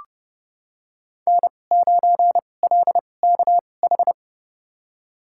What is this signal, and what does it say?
Morse code, heard as a keyed tone, about 30 wpm, spaced as for 22 wpm, 700 Hz. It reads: N9LKH
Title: USA Call Signs - 5 Characters - 2x Character Spacing - 30wpm